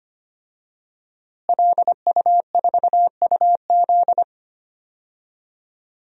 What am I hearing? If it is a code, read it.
Morse code, heard as a keyed tone, about 25 wpm, 700 Hz. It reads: LU4UZ
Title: DXCC-100 Call Signs - 5 Characters - Repeated 2x - 25wpm